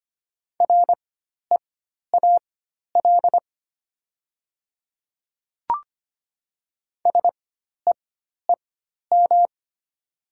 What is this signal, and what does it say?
Morse code, heard as a keyed tone, about 25 wpm, spaced as for 12 wpm, 700 Hz. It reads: REAL SEEM
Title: 4x Character Spacing - Top 300 Words - 25wpm